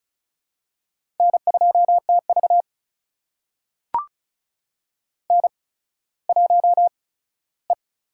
Morse code, heard as a keyed tone, about 36 wpm, spaced as for 17 wpm, 700 Hz. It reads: N2TV N1E
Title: USA Call Signs - 8x Character Spacing - 35wpm